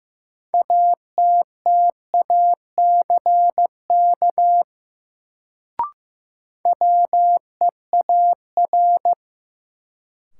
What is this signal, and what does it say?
Morse code, heard as a keyed tone, about 15 wpm, 700 Hz. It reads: ATTACK WEAR
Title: Sentences from Top 1500 Words - New Word Drill Alt - 15wpm